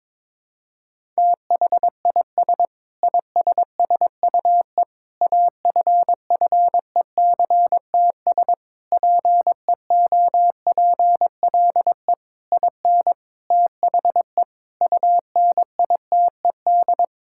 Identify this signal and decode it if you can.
Morse code, heard as a keyed tone, about 21 wpm, 700 Hz. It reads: THIS ISSUE AFFECTS PEOPLE IN THE UNITED
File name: practice / 22wpm / Sentences with 8-Word Phrases - 22wpm